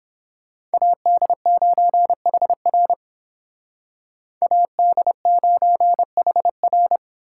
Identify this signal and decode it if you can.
Morse code, heard as a keyed tone, about 29 wpm, 700 Hz. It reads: AD9HR AD9HR